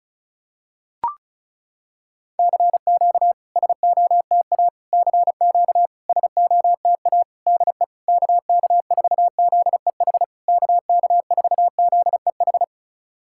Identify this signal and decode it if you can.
Morse code, heard as a keyed tone, about 35 wpm, 700 Hz. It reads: CQ SOTA CQ SOTA DE KK4ZEH KK4ZEH
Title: SOTA QSO Practice 35wpm